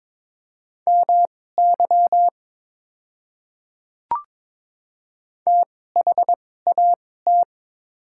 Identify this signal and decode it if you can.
Morse code, heard as a keyed tone, about 22 wpm, spaced as for 16 wpm, 700 Hz. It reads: MY THAT
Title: 2x Character Spacing - Top 200 Words - 22wpm